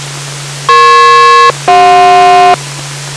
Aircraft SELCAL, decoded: EP-CJ